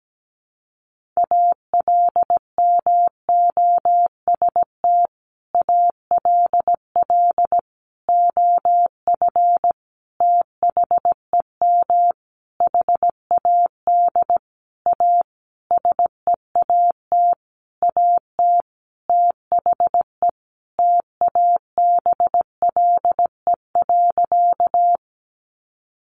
Morse code, heard as a keyed tone, about 17 wpm, 700 Hz. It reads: ALMOST ALL OF THEM HAD A SEAT AT THE TABLE.
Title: Sentences from Top 1500 Words 17wpm